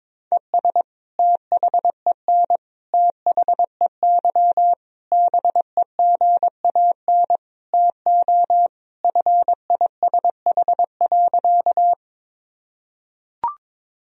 Morse code, heard as a keyed tone, about 22 wpm, 700 Hz. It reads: ES THEN THEY BEGAN TO FISH.
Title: Sentences from Top 300 Words and Abbreviations 22wpm